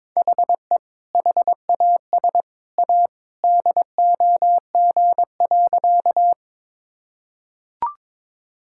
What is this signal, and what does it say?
Morse code, heard as a keyed tone, about 22 wpm, 700 Hz. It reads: HE HAS A DOG.
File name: practice / 22wpm / Sentences with 2-Word Phrases - 22wpm